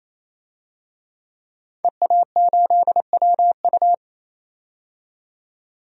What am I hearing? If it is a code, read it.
Morse code, heard as a keyed tone, about 28 wpm, 700 Hz. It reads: EA8WU
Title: DXCC-100 Call Signs - 5 Characters - 28wpm